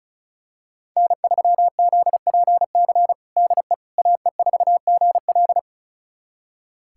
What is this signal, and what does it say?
Morse code, heard as a keyed tone, about 36 wpm, 700 Hz. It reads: N3ZPC DE AE4GL
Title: USA Callsign DE Callsign - 5 Characters - 35wpm